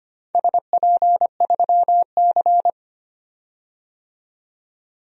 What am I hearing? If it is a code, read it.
Morse code, heard as a keyed tone, about 25 wpm, 700 Hz. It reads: SP3C
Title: DXCC-100 Call Signs - 4 Characters - 25wpm